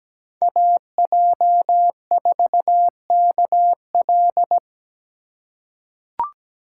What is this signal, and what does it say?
Morse code, heard as a keyed tone, about 17 wpm, 700 Hz. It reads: AJ4KL
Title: USA Call Signs - 3x Character Spacing - 17wpm